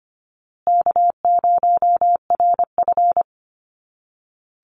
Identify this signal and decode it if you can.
Morse code, heard as a keyed tone, about 25 wpm, 700 Hz. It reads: K0RF
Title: Easy Call Signs Repeated 3x - 25wpm